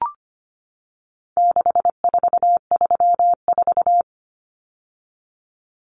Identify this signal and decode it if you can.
Morse code, heard as a keyed tone, about 25 wpm, 700 Hz. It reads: 6434
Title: Numbers - 4-Digit 25wpm